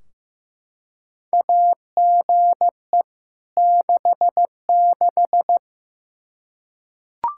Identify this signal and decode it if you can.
Morse code, heard as a keyed tone, about 15 wpm, 700 Hz. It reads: AGE 66
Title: Practice QSO - Exchange 3 - More Info - 15wpm